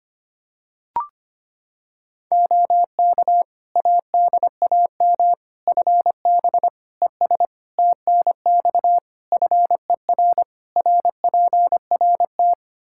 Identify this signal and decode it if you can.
Morse code, heard as a keyed tone, about 25 wpm, 700 Hz. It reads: OK ADAM FB ES TNX FER RPRT
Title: Practice QSO - Exchange 2 - Info - 25wpm